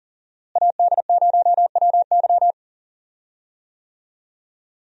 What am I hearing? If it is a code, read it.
Morse code, heard as a keyed tone, about 40 wpm, 700 Hz. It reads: AD0WY
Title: USA Call Signs - 5 Characters - Repeated 3x - 40wpm